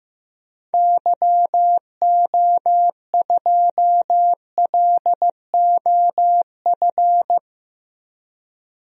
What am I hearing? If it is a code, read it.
Morse code, heard as a keyed tone, about 15 wpm, 700 Hz. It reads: YO2LOF
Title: DXCC-100 Call Signs - 6 Characters - 15wpm